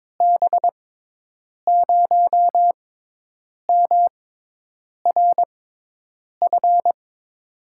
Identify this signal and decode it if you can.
Morse code, heard as a keyed tone, about 22 wpm, spaced as for 8 wpm, 700 Hz. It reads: B0MRF
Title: USA Call Signs - 6 Characters - 6x Character Spacing - 22wpm